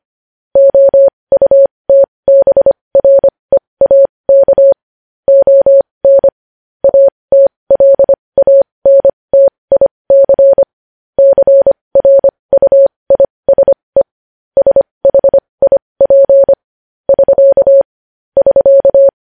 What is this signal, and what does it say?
Morse code, heard as a keyed tone, about 25 wpm, spaced as for 20 wpm, 550 Hz. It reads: OUTBREAK ON ATLANTIC CRUISE SHIP <SK> <SK>
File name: podcast / News headlines-1x-25x20